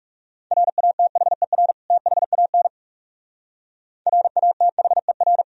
Morse code, heard as a keyed tone, about 36 wpm, 700 Hz. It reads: RATHER THAN RATHER
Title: Speed-Racing - Sets of 2 Words 30wpm